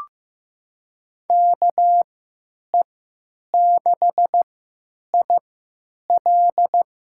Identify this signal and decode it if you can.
Morse code, heard as a keyed tone, about 16 wpm, spaced as for 9 wpm, 700 Hz. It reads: KE6IL
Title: USA Call Signs - 6 Characters - 3x Character Spacing - 15wpm